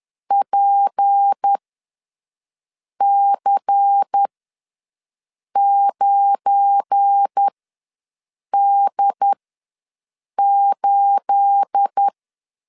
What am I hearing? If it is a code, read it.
Morse code, heard as a keyed tone, about 10 wpm, spaced as for 5 wpm, 800 Hz. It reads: PC9D8
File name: LEZIONI / 9 28 ws 10wpm 4124